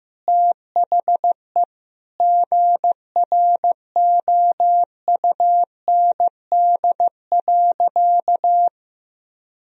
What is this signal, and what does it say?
Morse code, heard as a keyed tone, about 14 wpm, 700 Hz. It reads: THE GROUND.
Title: Sentences from Top 1000 Words - Review Part 2 - 15wpm